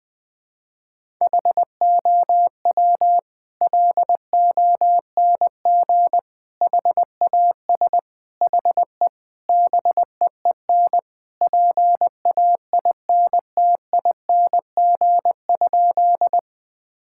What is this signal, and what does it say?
Morse code, heard as a keyed tone, about 21 wpm, 700 Hz. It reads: HOW LONG HAS HE BEEN PAINTING?
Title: Sentences from Top 1000 Words - Review Part 6 - 20wpm